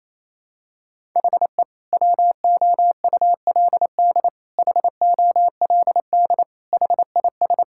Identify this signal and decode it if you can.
Morse code, heard as a keyed tone, about 28 wpm, 700 Hz. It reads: HE WOULD HOLD HIS
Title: Sentences from Top 1000 Words - Review Part 6 - 28wpm